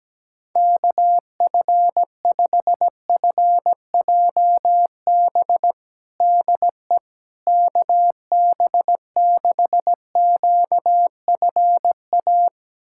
Morse code, heard as a keyed tone, about 17 wpm, 700 Hz. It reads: KF5FJB DE KB6QFA
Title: USA Callsign DE Callsign - 6 Characters - 17wpm